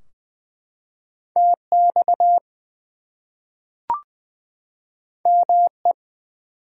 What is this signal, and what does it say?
Morse code, heard as a keyed tone, about 20 wpm, 700 Hz. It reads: TX ME
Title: US States and Canadian Provinces 20wpm